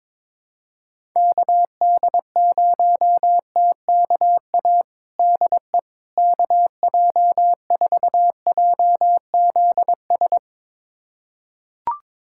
Morse code, heard as a keyed tone, about 22 wpm, 700 Hz. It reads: KD0TKA DE KJ4JZS
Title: USA Callsign DE Callsign - 6 Characters - 22wpm